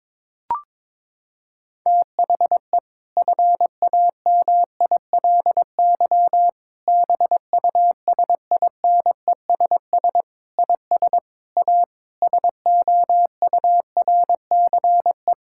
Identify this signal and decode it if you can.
Morse code, heard as a keyed tone, about 22 wpm, 700 Hz. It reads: THE FAMILY BUSINESS IS A SOURCE
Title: Sentences from Top 1000 Words - Review Part 6 - 22wpm